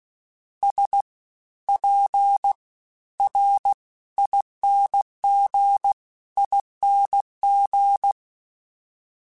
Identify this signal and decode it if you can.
Morse code, heard as a keyed tone, about 16 wpm, spaced as for 12 wpm, 800 Hz. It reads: SPRINGING
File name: Special INGs-mp3s / Special INGs-mp3s / SlowToFaster ing @15